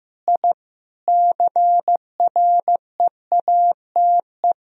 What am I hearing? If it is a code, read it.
Morse code, heard as a keyed tone, about 15 wpm, 700 Hz. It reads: I CREATE
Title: Sentences from Top 1000 Words - Review Part 3 - 15wpm